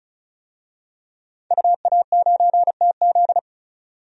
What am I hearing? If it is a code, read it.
Morse code, heard as a keyed tone, about 36 wpm, 700 Hz. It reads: UA9TZ
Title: DXCC-100 Call Signs - 5 Characters - 35wpm